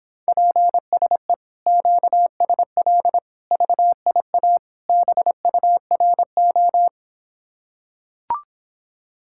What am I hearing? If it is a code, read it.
Morse code, heard as a keyed tone, about 26 wpm, 700 Hz. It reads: PSE QSL VIA BURO